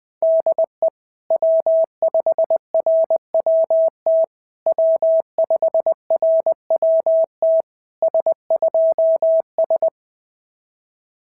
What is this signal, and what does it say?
Morse code, heard as a keyed tone, about 20 wpm, 650 Hz. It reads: DE W5RWT W5RWT S2S
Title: SOTA QSO Practice 20wpm